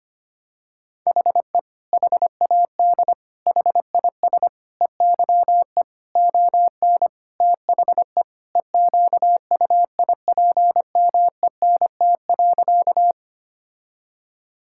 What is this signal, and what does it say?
Morse code, heard as a keyed tone, about 25 wpm, 700 Hz. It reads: HE HAD HIS EYE ON THE EQUIPMENT.
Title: Sentences from Top 1000 Words - Review Part 5 - 25wpm